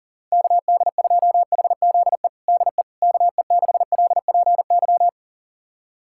Morse code, heard as a keyed tone, about 37 wpm, 700 Hz. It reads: KD2HZE DE KE6LPY
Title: USA Callsign DE Callsign - 6 Characters - 40wpm